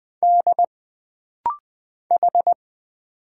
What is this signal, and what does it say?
Morse code, heard as a keyed tone, about 20 wpm, 700 Hz. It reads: D H
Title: Single Letters - Rapid-Fire 20wpm